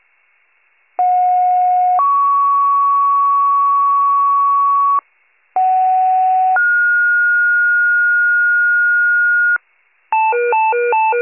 Two-tone page, heard as a one-second tone, then a three-second tone, about 730 then 1120 Hz, about 750 then 1430 Hz, with noise. It is heard from 1.0 to 9.6 s.